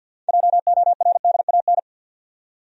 Morse code, heard as a keyed tone, about 53 wpm, 700 Hz. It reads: JORDAN
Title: USA Common Names - 50wpm